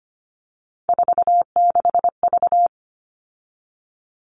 Morse code, heard as a keyed tone, about 25 wpm, 700 Hz. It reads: 46V
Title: All Cut Numbers - 3-Digit 25wpm